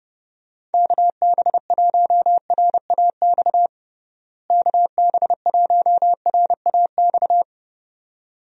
Morse code, heard as a keyed tone, about 32 wpm, 700 Hz. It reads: KB1RAX KB1RAX
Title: USA Call Signs - 6 Characters - Repeated 2x - 30wpm